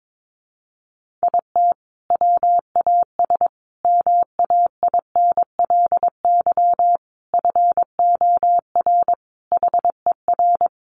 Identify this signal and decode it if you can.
Morse code, heard as a keyed tone, about 22 wpm, 700 Hz. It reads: IT WAS MAINLY FOR HER
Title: Sentences from Top 1400 Words 22wpm